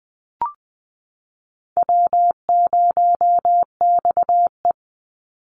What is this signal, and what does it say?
Morse code, heard as a keyed tone, about 20 wpm, 700 Hz. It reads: W0XE
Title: Easy Call Signs Repeated 3x - 20wpm